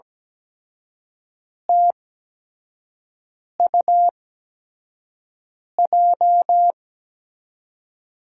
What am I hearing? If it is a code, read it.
Morse code, heard as a keyed tone, about 17 wpm, spaced as for 5 wpm, 700 Hz. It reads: TUJ
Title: USA Call Signs - 8x Character Spacing - 17wpm